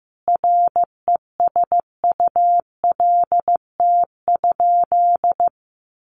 Morse code, heard as a keyed tone, about 15 wpm, 700 Hz. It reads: RESULT?
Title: Sentences from Top 1500 Words 15wpm